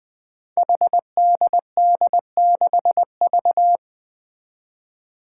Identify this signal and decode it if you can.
Morse code, heard as a keyed tone, about 21 wpm, 700 Hz. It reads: HDD6V
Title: Five Challenging Letter-Number Combinations 20wpm